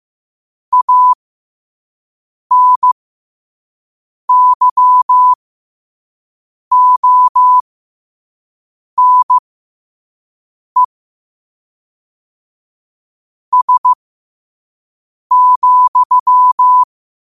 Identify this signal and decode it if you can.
Morse code, heard as a keyed tone, about 15 wpm, spaced as for 5 wpm, 1000 Hz. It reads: ANYONE S,
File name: 5wpm